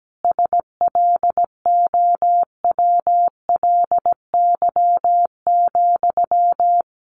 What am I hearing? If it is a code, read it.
Morse code, heard as a keyed tone, about 17 wpm, 700 Hz. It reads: SLOWLY,
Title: Sentences from Top 1100 Words 17wpm